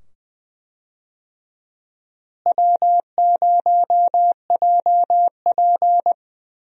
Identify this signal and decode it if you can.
Morse code, heard as a keyed tone, about 20 wpm, 700 Hz. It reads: W0JP
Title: USA Call Signs - 4 Characters - Repeated 2x - 20wpm